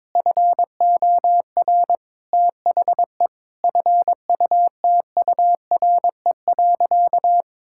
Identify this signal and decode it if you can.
Morse code, heard as a keyed tone, about 22 wpm, 700 Hz. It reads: FOR THE FUTURE.
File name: Difficult Sentences with Hints - 22wpm